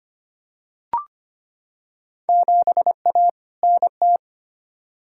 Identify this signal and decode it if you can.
Morse code, heard as a keyed tone, about 25 wpm, 700 Hz. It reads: 7A NT
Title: Field Day Exchanges Encore - 25wpm